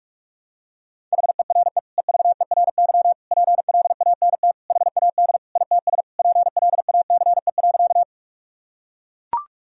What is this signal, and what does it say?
Morse code, heard as a keyed tone, about 41 wpm, 700 Hz. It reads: HERE EVERY PLANT HAD ITS PLACE.
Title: Sentences from Top 1000 Words - Review Part 1 - 45wpm